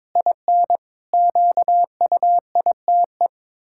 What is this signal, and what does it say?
Morse code, heard as a keyed tone, about 22 wpm, 700 Hz. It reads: IN QUITE
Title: Pangram Challenge v2 - 22wpm